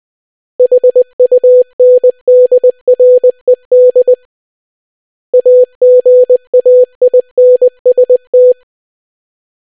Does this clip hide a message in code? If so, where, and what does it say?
Yes: Morse code from 0.6 to 8.5 s, heard as a keyed tone, about 20 wpm, 500 Hz. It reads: HUNDRED AGAINST